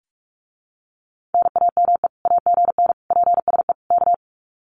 Morse code, heard as a keyed tone, about 46 wpm, 700 Hz. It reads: NAME AGN PSE K
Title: Common QSO Phrases Encore 45wpm